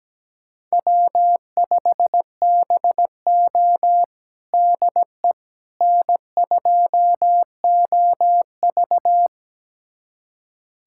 Morse code, heard as a keyed tone, about 17 wpm, 700 Hz. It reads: W5BO DE N2OV
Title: USA Callsign DE Callsign - 4 Characters - 17wpm